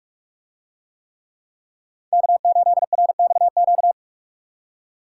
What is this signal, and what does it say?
Morse code, heard as a keyed tone, about 43 wpm, 700 Hz. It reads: K8RXQ